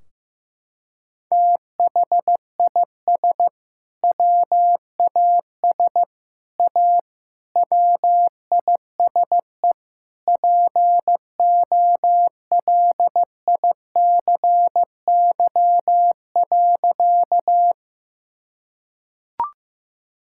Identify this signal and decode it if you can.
Morse code, heard as a keyed tone, about 15 wpm, 700 Hz. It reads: THIS WAS A WISE POLICY.
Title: Sentences from Top 1600 Words 15wpm